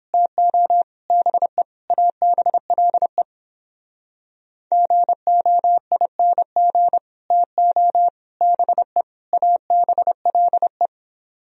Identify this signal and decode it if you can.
Morse code, heard as a keyed tone, about 29 wpm, 700 Hz. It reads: TO BE ABLE GOING TO BE ABLE